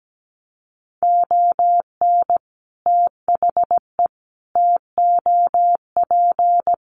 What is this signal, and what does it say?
Morse code, heard as a keyed tone, about 17 wpm, 700 Hz. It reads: ON THE TOP
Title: Sets of 5 Words Encore 17wpm